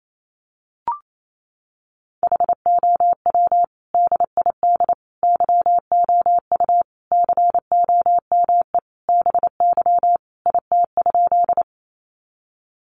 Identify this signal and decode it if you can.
Morse code, heard as a keyed tone, about 28 wpm, 700 Hz. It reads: HOW DID YOU COME BY IT?
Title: Sentences from Top 100 Words 28wpm